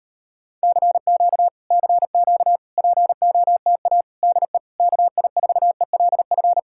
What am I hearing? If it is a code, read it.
Morse code, heard as a keyed tone, about 38 wpm, 700 Hz. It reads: CQ CQ POTA DE KI4ELF